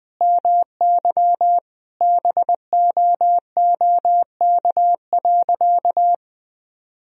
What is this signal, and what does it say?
Morse code, heard as a keyed tone, about 21 wpm, 700 Hz. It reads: MY BOOK.
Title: Sentences from Top 1000 Words - Review Part 3 - 20wpm